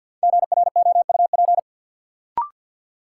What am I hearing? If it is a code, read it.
Morse code, heard as a keyed tone, about 47 wpm, 700 Hz. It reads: GROUP
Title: Sentences from Top 1000 Words - Review Part 1 - New Word Drill - 50wpm